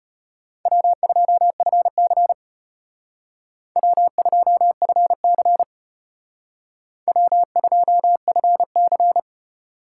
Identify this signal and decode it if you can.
Morse code, heard as a keyed tone, about 33 wpm, 700 Hz. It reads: W2FC W2FC W2FC